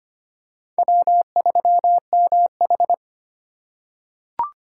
Morse code, heard as a keyed tone, about 25 wpm, 700 Hz. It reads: W3MH
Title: USA Call Signs - 4 Characters - 3x Character Spacing - 25wpm